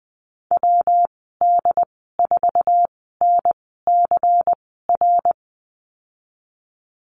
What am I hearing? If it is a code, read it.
Morse code, heard as a keyed tone, about 20 wpm, spaced as for 15 wpm, 700 Hz. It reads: WD4NCR
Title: 173 Lesson 28g - Number 9 - Call Signs 20wpm